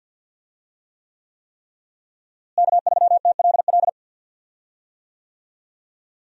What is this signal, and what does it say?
Morse code, heard as a keyed tone, about 46 wpm, 700 Hz. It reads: K3TLL
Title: USA Call Signs - 5 Characters - Repeated 3x - 50wpm